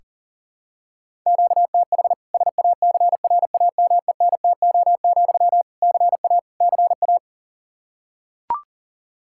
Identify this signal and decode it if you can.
Morse code, heard as a keyed tone, about 40 wpm, 700 Hz. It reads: QTH SACRAMENTO, CA CA
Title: K1USN SST QSO Practice 40wpm